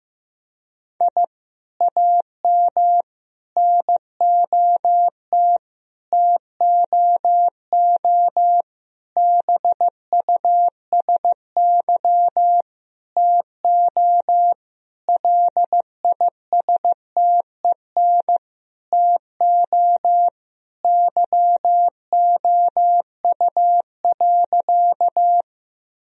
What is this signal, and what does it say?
Morse code, heard as a keyed tone, about 16 wpm, 700 Hz. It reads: I AM NOT TOO BUSY TO LISTEN TO YOU.
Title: Sentences from Top 1000 Words - Review Part 3 - 15wpm